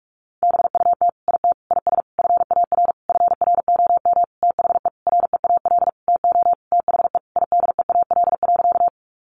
Morse code, heard as a keyed tone, about 44 wpm, 700 Hz. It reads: BUT IT IS FAR FROM THE REAL TO THE IDEAL.
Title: Sentences from Top 1400 Words 45wpm